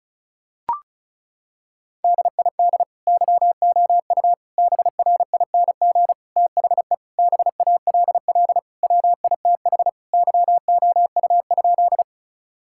Morse code, heard as a keyed tone, about 36 wpm, 700 Hz. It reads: DID YOU BRING THE BALL WITH YOU?
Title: Sentences from Top 1000 Words - Review Part 6 - 35wpm